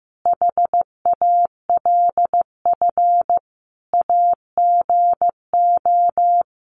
Morse code, heard as a keyed tone, about 15 wpm, 700 Hz. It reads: HALF AGO
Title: Sets of 5 Words Encore 15wpm